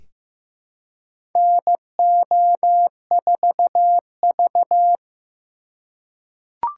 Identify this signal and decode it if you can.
Morse code, heard as a keyed tone, about 15 wpm, 700 Hz. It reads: NO4V
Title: USA Call Signs - 4 Characters - 2x Character Spacing - 15wpm